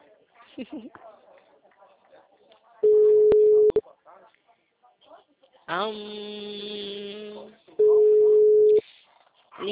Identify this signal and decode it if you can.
Ringback tone (Europe, 425 Hz), with speech.